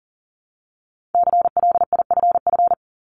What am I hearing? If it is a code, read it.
Morse code, heard as a keyed tone, about 42 wpm, 700 Hz. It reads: CLIFF
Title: Multi-letter Phonemes 40wpm